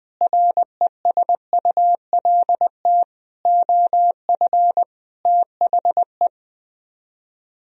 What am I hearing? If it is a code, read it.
Morse code, heard as a keyed tone, about 20 wpm, 700 Hz. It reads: RESULT OF THE
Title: Speed-Racing - Sets of 5 Words 20wpm